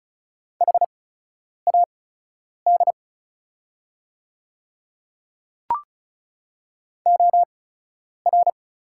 Morse code, heard as a keyed tone, about 36 wpm, spaced as for 10 wpm, 700 Hz. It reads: HAD OR
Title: 8x Character Spacing - Top 100 Words - 35wpm